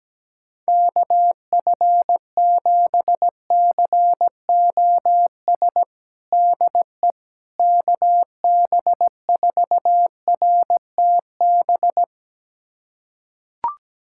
Morse code, heard as a keyed tone, about 17 wpm, 700 Hz. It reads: KF7COS DE KB4RTB
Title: USA Callsign DE Callsign - 6 Characters - 17wpm